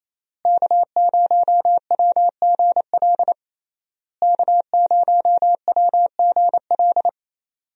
Morse code, heard as a keyed tone, about 27 wpm, 700 Hz. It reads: K0WGL K0WGL
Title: USA Call Signs - 5 Characters - Repeated 3x - 28wpm